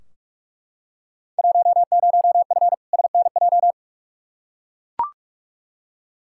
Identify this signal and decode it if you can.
Morse code, heard as a keyed tone, about 42 wpm, 700 Hz. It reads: 10F SNJ
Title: Field Day Exchanges Encore - 45wpm